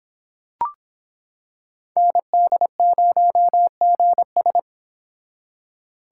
Morse code, heard as a keyed tone, about 25 wpm, 700 Hz. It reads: ND0GS